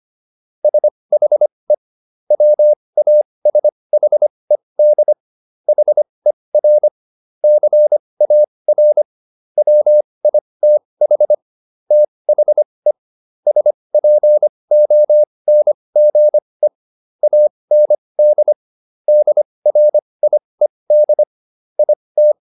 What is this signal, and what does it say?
Morse code, heard as a keyed tone, about 25 wpm, spaced as for 20 wpm, 600 Hz. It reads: SHE WASHED HER CAR WITH THE SPONGE AND DRIED IT